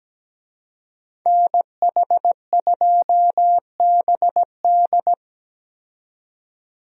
Morse code, heard as a keyed tone, about 17 wpm, 700 Hz. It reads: NH2BD
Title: USA Call Signs - 5 Characters - 17wpm